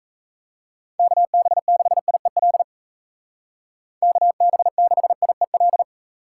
Morse code, heard as a keyed tone, about 36 wpm, 700 Hz. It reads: KB6IEL KB6IEL